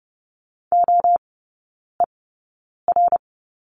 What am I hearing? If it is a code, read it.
Morse code, heard as a keyed tone, about 30 wpm, spaced as for 9 wpm, 700 Hz. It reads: OER
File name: course / 30:10wpm / 087 Lesson 15e - Number 5 - Three Characters 30wpm